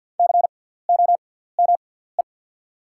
Morse code, heard as a keyed tone, about 48 wpm, spaced as for 17 wpm, 700 Hz. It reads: COME
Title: 6x Character Spacing - Top 100 Words - 50wpm